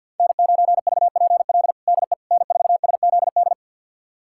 Morse code, heard as a keyed tone, about 53 wpm, 700 Hz. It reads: N9VPL DE N4SZD
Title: USA Callsign DE Callsign - 5 Characters - 50wpm